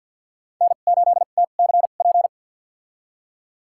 Morse code, heard as a keyed tone, about 48 wpm, spaced as for 36 wpm, 700 Hz. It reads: N8TCP
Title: USA Call Signs - 5 Characters - 2x Character Spacing - 50wpm